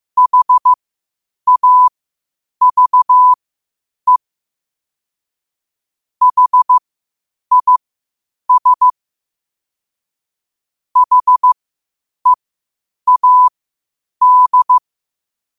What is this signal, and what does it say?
Morse code, heard as a keyed tone, about 15 wpm, spaced as for 8 wpm, 1000 Hz. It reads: HAVE HIS HEAD
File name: qotdfiles / 8wpm